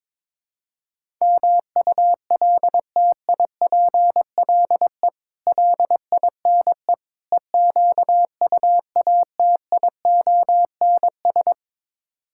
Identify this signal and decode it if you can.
Morse code, heard as a keyed tone, about 22 wpm, 700 Hz. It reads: MULTIPLE LINE EQUATIONS
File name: practice / 22wpm / Build A Phrase - 5-Word Phrases - 22wpm